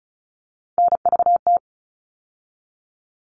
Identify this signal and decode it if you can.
Morse code, heard as a keyed tone, about 34 wpm, 700 Hz. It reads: NVT
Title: All Cut Numbers - 3-Digit 35wpm